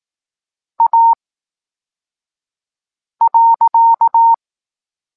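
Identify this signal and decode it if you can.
Morse code, heard as a keyed tone, about 18 wpm, spaced as for 4 wpm, 920 Hz. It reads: A.
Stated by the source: Evenly spaced Morse lesson text 21 for learning letters G and Z, from SP9MOA Niepołomicki Klub Krótkofalowców.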